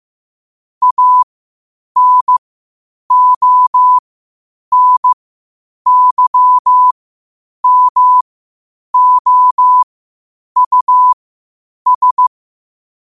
Morse code, heard as a keyed tone, about 15 wpm, spaced as for 8 wpm, 1000 Hz. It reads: ANONYMOUS
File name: qotdfiles / 8wpm